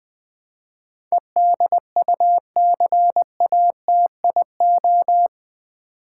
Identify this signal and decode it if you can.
Morse code, heard as a keyed tone, about 20 wpm, 700 Hz. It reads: EDUCATIO
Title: Build A Word - 4 Letters At A Time - 11-Letter Words - 20wpm